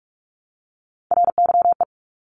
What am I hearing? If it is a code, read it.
Morse code, heard as a keyed tone, about 46 wpm, 700 Hz. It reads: RYE
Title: Multi-letter Phonemes Repeated 3x 45wpm